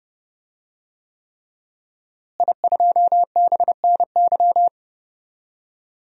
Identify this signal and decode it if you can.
Morse code, heard as a keyed tone, about 30 wpm, 700 Hz. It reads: I2BNY
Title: DXCC-100 Call Signs - 5 Characters - Repeated 3x - 30wpm